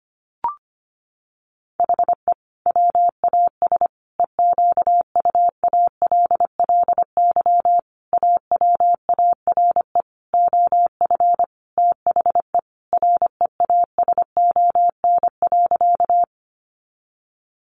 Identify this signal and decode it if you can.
Morse code, heard as a keyed tone, about 25 wpm, 700 Hz. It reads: HE WAS EQUALLY AWARE OF THE REASON.
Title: Sentences from Top 1400 Words 25wpm